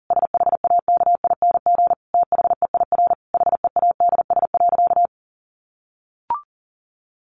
Morse code, heard as a keyed tone, about 40 wpm, 700 Hz. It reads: SHAKING THEIR HEADS.
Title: A Sound Of Thunder 40wpm